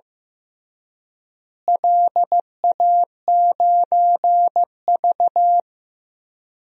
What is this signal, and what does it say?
Morse code, heard as a keyed tone, about 15 wpm, 700 Hz. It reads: LA9V
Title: DXCC-100 Call Signs - 4 Characters - Repeated 3x - 15wpm